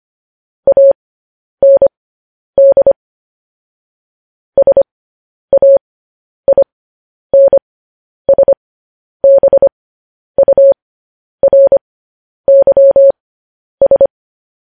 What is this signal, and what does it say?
Morse code, heard as a keyed tone, about 25 wpm, spaced as for 10 wpm, 550 Hz. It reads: AND SAINSBURYS
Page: news-headlines-in-morse-code-at-10-wpm